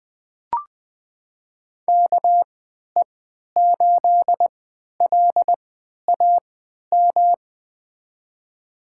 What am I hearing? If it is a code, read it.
Morse code, heard as a keyed tone, about 20 wpm, spaced as for 11 wpm, 700 Hz. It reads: KE8LAM
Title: USA Call Signs - 3x Character Spacing - 20wpm